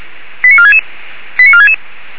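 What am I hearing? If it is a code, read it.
Five-tone selective call (ZVEI), heard as quick runs of five tones, tones of about 70 ms. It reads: 89466 89466
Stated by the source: ZVEI five-tone call 89332 received by SDR.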